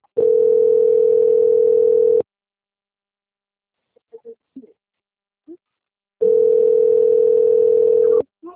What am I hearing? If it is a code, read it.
Ringback tone (North America), 440 + 480 Hz.
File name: out- - - - - .26357